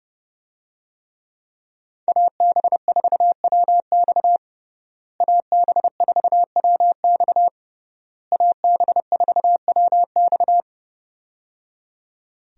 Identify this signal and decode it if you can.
Morse code, heard as a keyed tone, about 30 wpm, 700 Hz. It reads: AB4WX AB4WX AB4WX
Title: USA Call Signs - 5 Characters - Repeated 3x - 30wpm